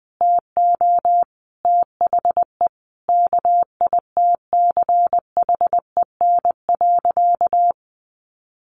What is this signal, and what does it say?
Morse code, heard as a keyed tone, about 20 wpm, 700 Hz. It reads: TO THE KITCHEN.
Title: Sentences from Top 1400 Words 20wpm